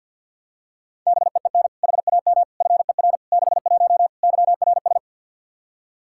Morse code, heard as a keyed tone, about 45 wpm, 700 Hz. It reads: BEEN HAM FER 61 YRS
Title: Practice QSO - Exchange 3 - More Info - 50wpm